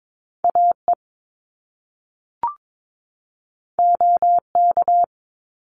Morse code, heard as a keyed tone, about 22 wpm, 700 Hz. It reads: AE OK
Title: Top Two Letter Combinations in Words 22wpm